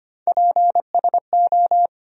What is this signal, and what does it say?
Morse code, heard as a keyed tone, about 25 wpm, 700 Hz. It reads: PSO
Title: USA Call Sign Postfixes - 3 Letters 25wpm